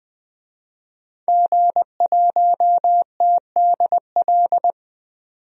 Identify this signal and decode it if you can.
Morse code, heard as a keyed tone, about 20 wpm, 700 Hz. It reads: G1TDL
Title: DXCC-100 Call Signs - 5 Characters - 20wpm